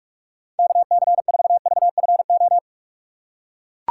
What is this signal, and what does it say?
Morse code, heard as a keyed tone, about 44 wpm, 700 Hz. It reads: KC4VFO
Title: USA Call Signs - 6 Characters - 4x Character Spacing - 45wpm